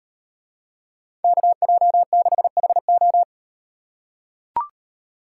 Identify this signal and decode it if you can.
Morse code, heard as a keyed tone, about 36 wpm, 700 Hz. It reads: KJ6HO